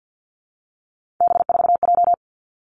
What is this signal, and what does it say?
Morse code, heard as a keyed tone, about 50 wpm, 700 Hz. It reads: B4J
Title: Three Challenging Letter-Number Combinations 50wpm